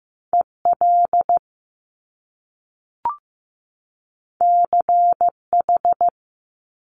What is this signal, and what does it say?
Morse code, heard as a keyed tone, about 15 wpm, 700 Hz. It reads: EL CH
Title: Top Two Letter Combinations in Words 15wpm